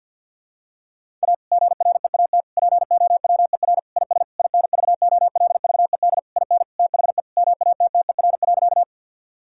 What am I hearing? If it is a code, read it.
Morse code, heard as a keyed tone, about 49 wpm, 700 Hz. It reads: A GREAT POWER IS INVOLVED IN THE MATTER.
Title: Sentences from Top 1000 Words - Review Part 5 - 50wpm